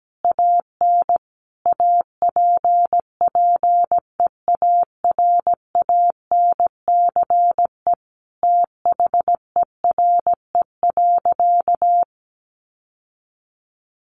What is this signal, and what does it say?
Morse code, heard as a keyed tone, about 17 wpm, 700 Hz. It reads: AN APPEARANCE THERE.
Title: Sentences from Top 1100 Words 17wpm